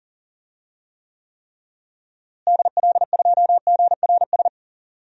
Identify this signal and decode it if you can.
Morse code, heard as a keyed tone, about 40 wpm, 700 Hz. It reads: DL2GRS
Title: DXCC-100 Call Signs - 6 Characters - Repeated 3x - 40wpm